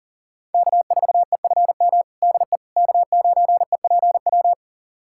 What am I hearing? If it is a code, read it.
Morse code, heard as a keyed tone, about 40 wpm, 700 Hz. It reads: K4EFM DE K9EPW
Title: USA Callsign DE Callsign - 5 Characters - 40wpm